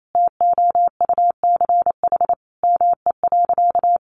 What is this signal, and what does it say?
Morse code, heard as a keyed tone, about 28 wpm, 700 Hz. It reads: TOUCH ME.
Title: Sentences from Top 1200 Words 28wpm